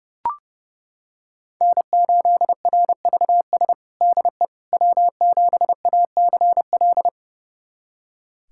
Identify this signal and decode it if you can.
Morse code, heard as a keyed tone, about 32 wpm, 700 Hz. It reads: N8RVS DE W7ACL
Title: USA Callsign DE Callsign - 5 Characters - 30wpm